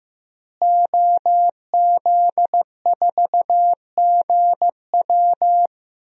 Morse code, heard as a keyed tone, about 15 wpm, 700 Hz. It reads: OZ4GW
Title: DXCC-100 Call Signs - 5 Characters - Repeated 2x - 15wpm